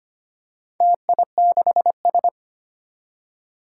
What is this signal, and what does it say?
Morse code, heard as a keyed tone, about 25 wpm, 700 Hz. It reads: TI6S
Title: Four Challenging Letter-Number Combinations 25wpm